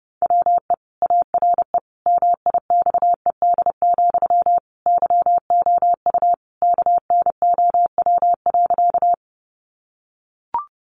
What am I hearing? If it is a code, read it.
Morse code, heard as a keyed tone, about 30 wpm, 700 Hz. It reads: WE ARE MIXED, YOU KNOW.
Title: Sentences from Top 1500 Words 30wpm